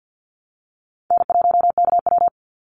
Morse code, heard as a keyed tone, about 49 wpm, 700 Hz. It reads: N1KW
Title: Call Signs Repeated 2x - 50wpm